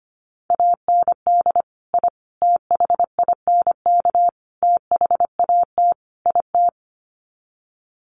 Morse code, heard as a keyed tone, about 25 wpm, 700 Hz. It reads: AND I THINK THAT IT
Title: Sets of 5 Words Encore2 25wpm